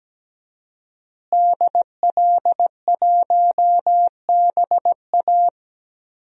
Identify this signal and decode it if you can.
Morse code, heard as a keyed tone, about 17 wpm, 700 Hz. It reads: DL1BA
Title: DXCC-100 Call Signs - 5 Characters - Repeated 2x - 17wpm